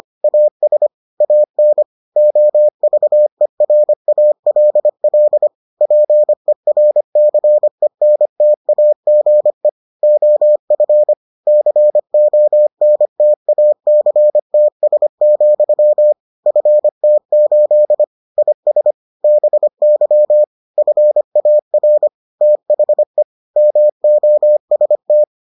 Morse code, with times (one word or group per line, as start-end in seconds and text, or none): 0.2-0.9 AS
1.2-1.8 AN
2.2-5.5 OVERALL
5.8-9.7 PERCENTAGE
10.0-11.1 OF
11.5-16.1 CONTACTS,
16.5-18.0 FT8
18.4-18.9 IS
19.2-20.4 BY
20.8-22.1 FAR
22.4-23.2 THE
23.6-25.2 MOST